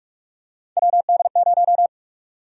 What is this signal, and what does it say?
Morse code, heard as a keyed tone, about 41 wpm, 700 Hz. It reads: WD0
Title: USA Call Sign Prefixes - Repeated 2x - 45wpm